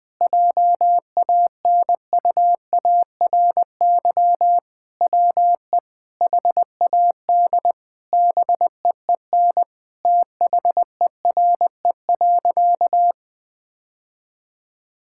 Morse code, heard as a keyed tone, about 21 wpm, 700 Hz. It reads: JANUARY WE HAD BEEN THERE.
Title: Sentences with 4-Word Phrases - Spoken Beginning - 20wpm